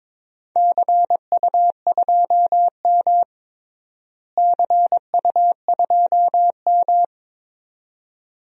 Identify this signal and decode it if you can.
Morse code, heard as a keyed tone, about 21 wpm, 700 Hz. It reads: CU2M CU2M
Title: DXCC-100 Call Signs - 4 Characters - Repeated 2x - 22wpm